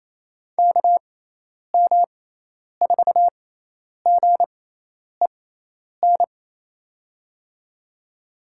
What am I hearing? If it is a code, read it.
Morse code, heard as a keyed tone, about 28 wpm, spaced as for 10 wpm, 700 Hz. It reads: KM4GEN
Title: USA Call Signs - 6 Characters - 6x Character Spacing - 28wpm